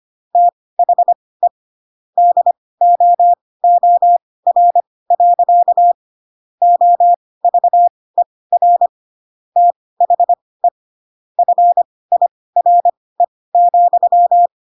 Morse code, heard as a keyed tone, about 25 wpm, spaced as for 18 wpm, 700 Hz. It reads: THE DOOR. OVER THE FIRE,
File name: Pinocchio ch3 12 25 ik1hge